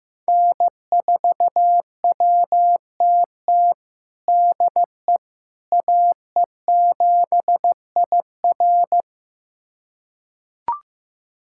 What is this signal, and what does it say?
Morse code, heard as a keyed tone, about 16 wpm, 700 Hz. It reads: N4WTT DE AE7IR
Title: USA Callsign DE Callsign - 5 Characters - 15wpm